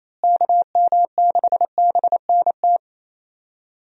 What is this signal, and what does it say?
Morse code, heard as a keyed tone, about 28 wpm, 700 Hz. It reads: KM6BNT
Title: USA Call Signs - 6 Characters - Repeated 2x - 28wpm